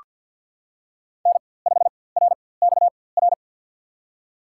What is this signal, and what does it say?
Morse code, heard as a keyed tone, about 46 wpm, spaced as for 23 wpm, 700 Hz. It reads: N5RXR
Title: USA Call Signs - 4x Character Spacing - 50wpm